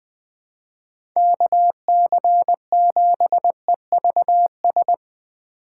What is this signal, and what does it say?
Morse code, heard as a keyed tone, about 20 wpm, 700 Hz. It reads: KC7EVS
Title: USA Call Signs - 6 Characters - 2x Character Spacing - 20wpm